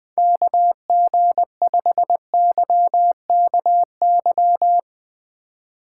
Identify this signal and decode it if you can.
Morse code, heard as a keyed tone, about 21 wpm, 700 Hz. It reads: KG5YKY
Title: Speed Racing - USA 2x3 Call Signs 20wpm